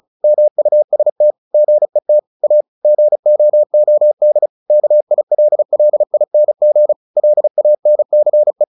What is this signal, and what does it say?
Morse code, heard as a keyed tone, about 34 wpm, 600 Hz. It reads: MUST GET A GOOD KILLING LANCE
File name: the old man and the sea 2 35wpm